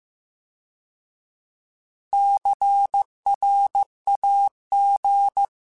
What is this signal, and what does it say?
Morse code, heard as a keyed tone, about 15 wpm, 800 Hz. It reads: CRAG